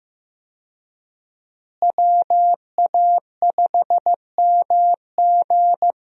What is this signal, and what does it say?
Morse code, heard as a keyed tone, about 15 wpm, 700 Hz. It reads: WA5MG
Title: USA Call Signs - 5 Characters - Repeated 2x - 15wpm